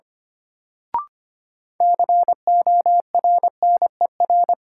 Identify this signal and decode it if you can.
Morse code, heard as a keyed tone, about 25 wpm, 700 Hz. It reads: CORNER
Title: Sentences from Top 1600 Words - New Word Drill - 25wpm